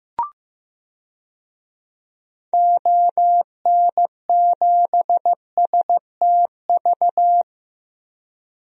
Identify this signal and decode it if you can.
Morse code, heard as a keyed tone, about 15 wpm, 700 Hz. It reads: ON7STV
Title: DXCC-100 Call Signs - 6 Characters - Repeated 2x - 15wpm